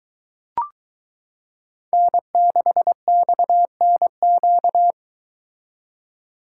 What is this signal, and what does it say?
Morse code, heard as a keyed tone, about 22 wpm, 700 Hz. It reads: N6XNQ